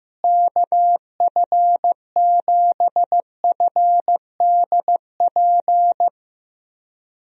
Morse code, heard as a keyed tone, about 15 wpm, 700 Hz. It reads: KF7FDP
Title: USA Call Signs - 6 Characters - Repeated 3x - 15wpm